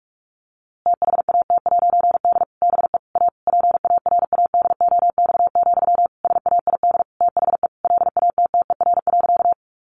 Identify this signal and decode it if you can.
Morse code, heard as a keyed tone, about 44 wpm, 700 Hz. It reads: THAT'D BE A PARADOX, SAID THE LATTER.
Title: A Sound Of Thunder 45wpm